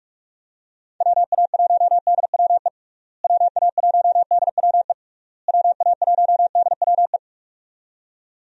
Morse code, heard as a keyed tone, about 46 wpm, 700 Hz. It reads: WA1DWE WA1DWE WA1DWE
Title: USA Call Signs - 6 Characters - Repeated 3x - 45wpm